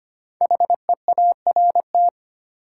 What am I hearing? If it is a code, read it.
Morse code, heard as a keyed tone, about 25 wpm, 700 Hz. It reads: HEART
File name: 5 Letter Words - First 3 Letters Given - 25wpm